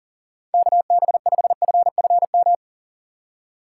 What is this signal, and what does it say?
Morse code, heard as a keyed tone, about 40 wpm, 700 Hz. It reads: KB5FFM
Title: USA Call Signs - 6 Characters - 3x Character Spacing - 40wpm